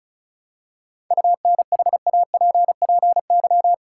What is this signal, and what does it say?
Morse code, heard as a keyed tone, about 36 wpm, 700 Hz. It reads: UNHAPPY
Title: Seven Letter Words - 35wpm